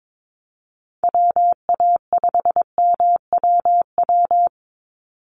Morse code, heard as a keyed tone, about 22 wpm, 700 Hz. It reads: WA5MWW
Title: USA Call Signs 22wpm